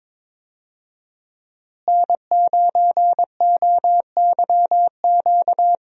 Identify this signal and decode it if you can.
Morse code, heard as a keyed tone, about 23 wpm, 700 Hz. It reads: N9OYQ
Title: USA Call Signs - 5 Characters - Repeated 3x - 22wpm